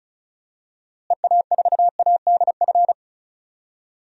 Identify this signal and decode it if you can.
Morse code, heard as a keyed tone, about 36 wpm, 700 Hz. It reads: EA4ADF
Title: DXCC-100 Call Signs - 6 Characters - Repeated 2x - 35wpm